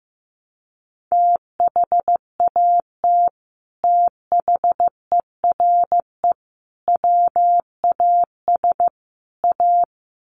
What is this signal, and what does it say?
Morse code, heard as a keyed tone, about 15 wpm, 700 Hz. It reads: THAT THERE WAS A
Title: Sets of 4 Words 15wpm